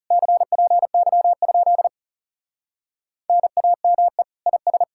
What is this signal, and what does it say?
Morse code, heard as a keyed tone, about 40 wpm, 700 Hz. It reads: CPY? NAME IS